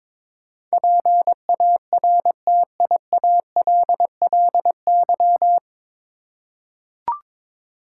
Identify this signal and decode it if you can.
Morse code, heard as a keyed tone, about 21 wpm, 700 Hz. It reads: PARTIALLY
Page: Morse Code Ninja practice 22wpm